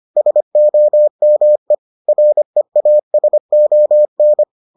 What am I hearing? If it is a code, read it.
Morse code, heard as a keyed tone, about 25 wpm, 600 Hz. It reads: SOME REASON